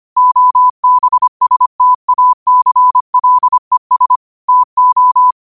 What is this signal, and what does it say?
Morse code, heard as a keyed tone, about 25 wpm, 1000 Hz. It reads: OBSTACLES TO
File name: qotdfiles / 25wpm